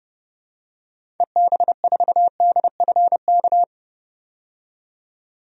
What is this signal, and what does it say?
Morse code, heard as a keyed tone, about 30 wpm, 700 Hz. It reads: EB4DFK
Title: DXCC-100 Call Signs - 6 Characters - 30wpm